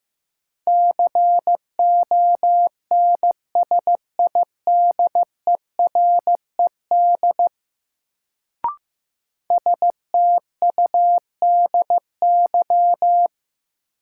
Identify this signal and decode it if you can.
Morse code, heard as a keyed tone, about 15 wpm, 700 Hz. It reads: CONSIDERED STUDY
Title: Sets of 5 Words - New Word Drill - 15wpm